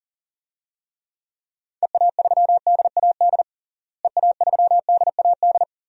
Morse code, heard as a keyed tone, about 40 wpm, 700 Hz. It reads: EA3DAD EA3DAD
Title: DXCC-100 Call Signs - 6 Characters - Repeated 3x - 40wpm